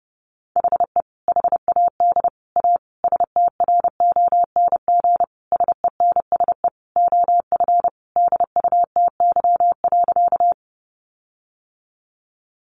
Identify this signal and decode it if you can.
Morse code, heard as a keyed tone, about 30 wpm, 700 Hz. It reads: HE HAD A STRONG SENSE OF DUTY.
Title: Sentences from Top 1500 Words 30wpm